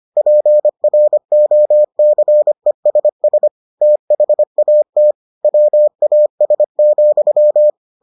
Morse code, heard as a keyed tone, about 25 wpm, 600 Hz. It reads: PROCESS THAT WAS,